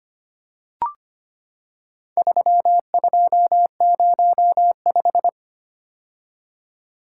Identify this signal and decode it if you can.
Morse code, heard as a keyed tone, about 25 wpm, 700 Hz. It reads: 3205
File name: practice / 25wpm / Four Letter-Number Combinations - 25wpm